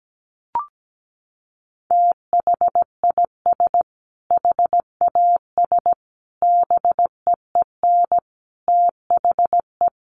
Morse code, heard as a keyed tone, about 17 wpm, 700 Hz. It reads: THIS HAS BEEN THE
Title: Sentences from Top 1400 Words 17wpm